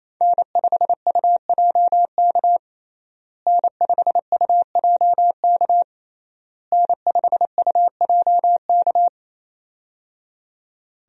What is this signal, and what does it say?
Morse code, heard as a keyed tone, about 28 wpm, 700 Hz. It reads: N5UJK N5UJK N5UJK
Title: USA Call Signs - 5 Characters - Repeated 3x - 28wpm